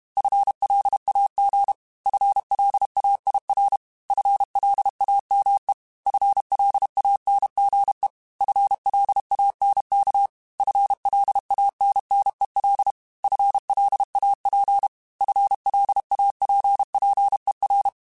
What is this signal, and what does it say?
Morse code, heard as a keyed tone, about 32 wpm, 800 Hz. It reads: FLAG FLAIR FLAME FLANGE FLANK FLANNEL FLAP FLAPPER
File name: FL Runs / FL Runs mp3s / FL Runs @30